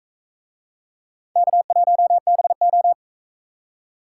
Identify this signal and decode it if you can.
Morse code, heard as a keyed tone, about 44 wpm, 700 Hz. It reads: K1BO